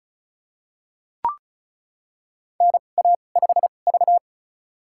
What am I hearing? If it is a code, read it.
Morse code, heard as a keyed tone, about 36 wpm, spaced as for 25 wpm, 700 Hz. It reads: NA5V
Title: USA Call Signs - 2x Character Spacing - 35wpm